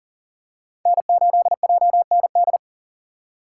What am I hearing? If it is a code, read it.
Morse code, heard as a keyed tone, about 40 wpm, 700 Hz. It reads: N8JND
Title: USA Call Signs - 5 Characters - 4x Character Spacing - 40wpm